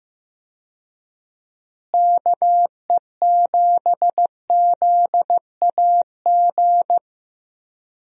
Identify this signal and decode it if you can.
Morse code, heard as a keyed tone, about 15 wpm, 700 Hz. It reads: KE7ZAG
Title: USA Call Signs - 6 Characters - Repeated 3x - 15wpm